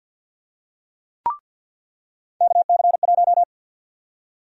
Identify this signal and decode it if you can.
Morse code, heard as a keyed tone, about 49 wpm, 700 Hz. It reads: KC1
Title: USA Call Sign Prefixes - 50wpm